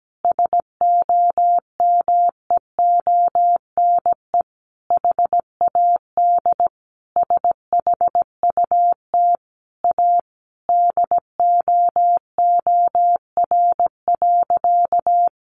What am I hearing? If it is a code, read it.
Morse code, heard as a keyed tone, about 17 wpm, 700 Hz. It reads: SOMEONE HAD SHUT A DOOR.
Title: A Sound Of Thunder 17wpm